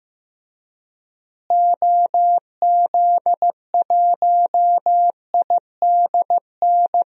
Morse code, heard as a keyed tone, about 16 wpm, 700 Hz. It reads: OZ1IDN
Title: DXCC-100 Call Signs - 6 Characters - Repeated 3x - 15wpm